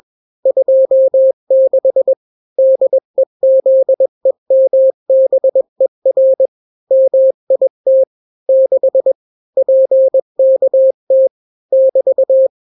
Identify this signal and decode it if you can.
Morse code, heard as a keyed tone, about 21 wpm, 530 Hz. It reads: 26 DEZEMBER MIT 6 PKT <BT>
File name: daily / 20wpm